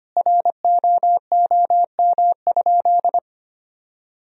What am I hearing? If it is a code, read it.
Morse code, heard as a keyed tone, about 25 wpm, 700 Hz. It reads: ROOM?
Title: Sentences from Top 300 Words and Abbreviations 25wpm